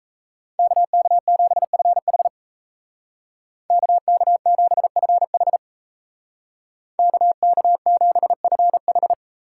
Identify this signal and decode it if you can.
Morse code, heard as a keyed tone, about 36 wpm, 700 Hz. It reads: KK7FH KK7FH KK7FH